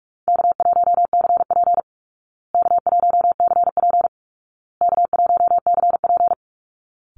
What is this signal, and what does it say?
Morse code, heard as a keyed tone, about 44 wpm, 700 Hz. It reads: K1CP K1CP K1CP
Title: Call Signs Repeated 2x - 45wpm